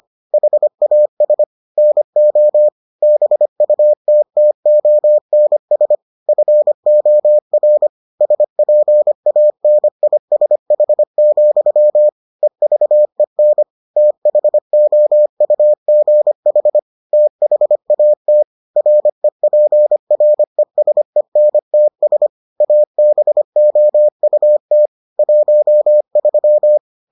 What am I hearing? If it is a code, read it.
Morse code, heard as a keyed tone, about 25 wpm, 600 Hz. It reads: HAS NO BUTTONS FOR SPANISH, EVEN THOUGH THAT REPRESENTS ABOUT 13